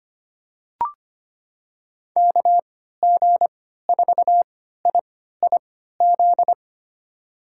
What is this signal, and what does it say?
Morse code, heard as a keyed tone, about 25 wpm, spaced as for 14 wpm, 700 Hz. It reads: KG4IIZ
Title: USA Call Signs - 6 Characters - 3x Character Spacing - 25wpm